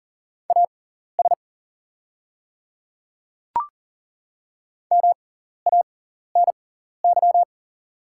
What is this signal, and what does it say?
Morse code, heard as a keyed tone, about 40 wpm, spaced as for 14 wpm, 690 Hz. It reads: AS MANY
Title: 6x Character Spacing - Top 500 Words - 40wpm